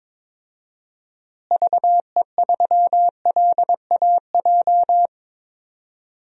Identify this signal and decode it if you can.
Morse code, heard as a keyed tone, about 22 wpm, 700 Hz. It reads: VE3LAJ
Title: DXCC-100 Call Signs - 6 Characters - Repeated 2x - 22wpm